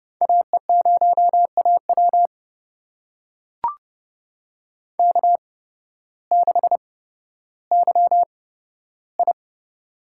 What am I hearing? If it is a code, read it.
Morse code, heard as a keyed tone, about 30 wpm, spaced as for 14 wpm, 700 Hz. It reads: AE0AW K6YI
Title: USA Call Signs - 8x Character Spacing - 30wpm